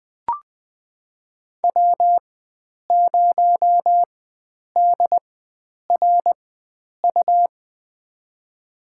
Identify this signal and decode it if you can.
Morse code, heard as a keyed tone, about 20 wpm, spaced as for 9 wpm, 700 Hz. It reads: W0DRU
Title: USA Call Signs - 5 Characters - 4x Character Spacing - 20wpm